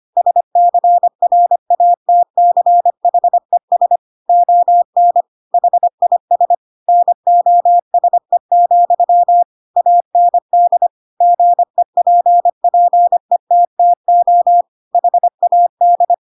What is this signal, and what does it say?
Morse code, heard as a keyed tone, about 25 wpm, 700 Hz. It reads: SCRATCHES ON HIS NOSE, AND GEPPETTO HAD
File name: Pinocchio / Pinocchio ch2 25 25 ik1hge